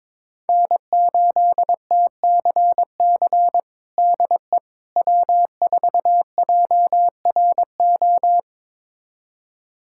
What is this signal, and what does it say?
Morse code, heard as a keyed tone, about 22 wpm, 700 Hz. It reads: N8TCC DE W4JRO
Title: USA Callsign DE Callsign - 5 Characters - 22wpm